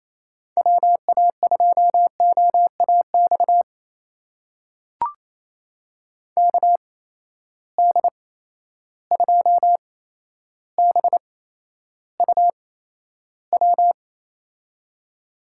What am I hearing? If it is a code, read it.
Morse code, heard as a keyed tone, about 28 wpm, spaced as for 12 wpm, 700 Hz. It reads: WA2OAX KD2BUW
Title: USA Call Signs - 6 Characters - 8x Character Spacing - 28wpm